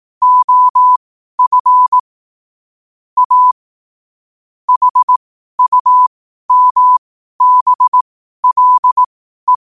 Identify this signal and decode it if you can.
Morse code, heard as a keyed tone, about 18 wpm, spaced as for 12 wpm, 1000 Hz. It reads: OF A HUMBLE
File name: 12wpm